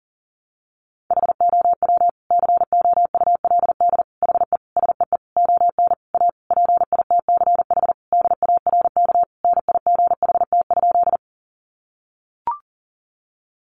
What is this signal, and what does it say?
Morse code, heard as a keyed tone, about 40 wpm, 700 Hz. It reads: HOW COULD HE SEE ON A PITCH DARK NIGHT?
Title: Sentences from Top 1300 Words 40wpm